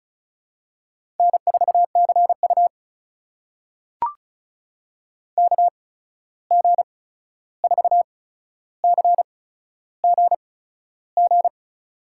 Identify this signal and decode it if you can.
Morse code, heard as a keyed tone, about 36 wpm, spaced as for 13 wpm, 700 Hz. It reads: N4CU KG4CGG
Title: USA Call Signs - 8x Character Spacing - 35wpm